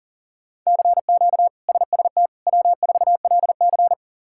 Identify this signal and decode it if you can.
Morse code, heard as a keyed tone, about 40 wpm, 700 Hz. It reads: CQ SST W4LC
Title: K1USN SST QSO Practice - 40wpm